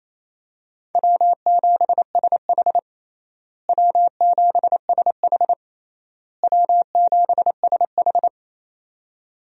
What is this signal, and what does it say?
Morse code, heard as a keyed tone, about 28 wpm, 700 Hz. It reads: W7SH W7SH W7SH
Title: USA Call Signs - 4 Characters - Repeated 3x - 28wpm